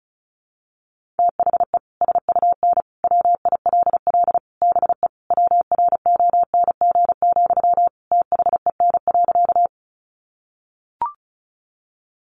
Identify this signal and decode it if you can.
Morse code, heard as a keyed tone, about 34 wpm, 700 Hz. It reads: THE SUN WILL BE WRONG, THEN.
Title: Sentences from Top 900 Words 35wpm